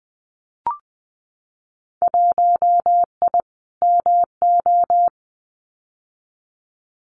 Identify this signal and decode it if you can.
Morse code, heard as a keyed tone, about 20 wpm, 700 Hz. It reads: 1I MO
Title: Winter Field Day Exchanges Encore 20wpm